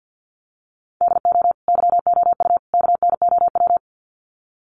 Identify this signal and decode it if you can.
Morse code, heard as a keyed tone, about 50 wpm, 700 Hz. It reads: DO YOU KNOW
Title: Sets of 3 Words 50wpm